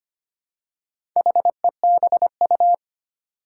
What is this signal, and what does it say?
Morse code, heard as a keyed tone, about 25 wpm, 700 Hz. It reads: HEBU
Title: Four Challenging Letter-Number Combinations 25wpm